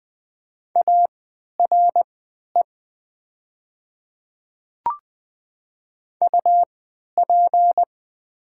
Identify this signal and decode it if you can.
Morse code, heard as a keyed tone, about 20 wpm, spaced as for 11 wpm, 700 Hz. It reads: ARE UP
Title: 3x Character Spacing - Top 100 Words - 20wpm